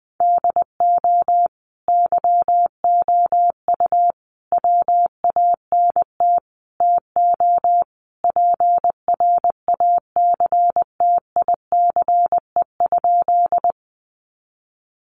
Morse code, heard as a keyed tone, about 20 wpm, 700 Hz. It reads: DO YOU WANT TO PRACTICE?
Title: Sentences from Top 700 Words 20wpm